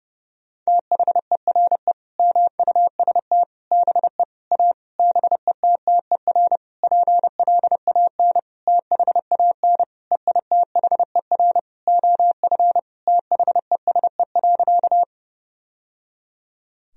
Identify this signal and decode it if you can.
Morse code, heard as a keyed tone, about 30 wpm, 700 Hz. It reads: THERE MUST BE A BETTER PLAN THAN EITHER OF THESE.
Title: Sentences from Top 1000 Words - Review Part 4 - 30wpm